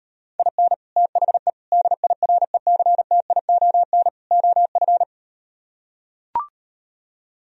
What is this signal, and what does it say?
Morse code, heard as a keyed tone, about 38 wpm, 700 Hz. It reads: IN THE DIRECTION OF